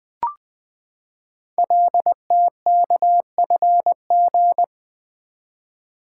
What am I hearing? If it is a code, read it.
Morse code, heard as a keyed tone, about 20 wpm, 700 Hz. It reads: LTKFG
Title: Code Groups - 5 Letters - 20wpm